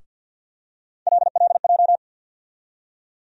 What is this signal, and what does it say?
Morse code, heard as a keyed tone, about 52 wpm, 700 Hz. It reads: LLJ
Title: USA Call Sign Postfixes - 3 Letters 50wpm